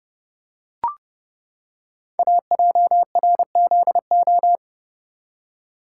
Morse code, heard as a keyed tone, about 30 wpm, 700 Hz. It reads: AJRZO
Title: Code Groups - 5 Letter:Numbers - 30wpm